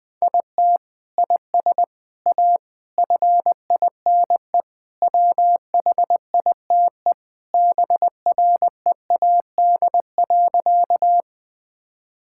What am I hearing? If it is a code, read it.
Morse code, heard as a keyed tone, about 20 wpm, 700 Hz. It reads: IT IS A FINE WHITE BREAD.
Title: Sentences from Top 1600 Words 20wpm